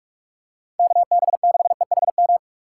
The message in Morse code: KB6EHM

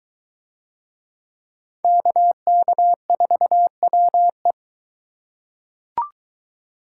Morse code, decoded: KK4WE

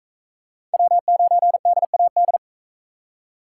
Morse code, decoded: W9DAD

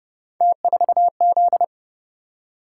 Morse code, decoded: T4Z

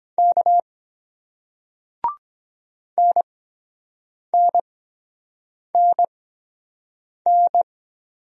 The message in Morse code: K NNNN

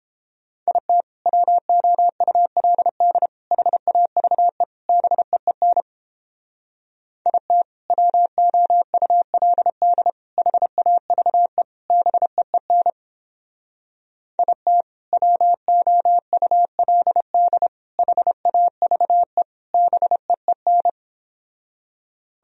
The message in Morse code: IT WOULD HAVE BEEN IT WOULD HAVE BEEN IT WOULD HAVE BEEN

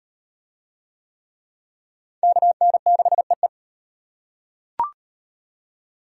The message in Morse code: KN6EE

